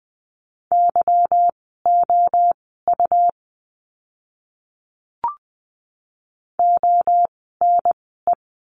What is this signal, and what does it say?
Morse code, heard as a keyed tone, about 20 wpm, spaced as for 14 wpm, 700 Hz. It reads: YOU ONE
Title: 192 Lesson 31f - Letter v - Words 20wpm